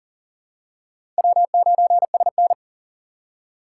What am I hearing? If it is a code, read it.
Morse code, heard as a keyed tone, about 40 wpm, 700 Hz. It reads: W9SN
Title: USA Call Signs - 4x Character Spacing - 40wpm